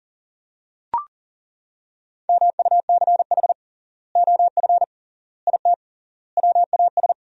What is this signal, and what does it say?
Morse code, heard as a keyed tone, about 40 wpm, 700 Hz. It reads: MUCH OF IT WAS
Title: 3x Word-Spacing - Sentences from Top 400 Words - 40wpm